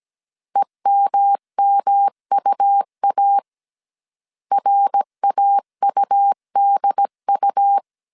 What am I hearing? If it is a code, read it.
Morse code, heard as a keyed tone, about 17 wpm, 800 Hz. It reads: EMMUA RAUDU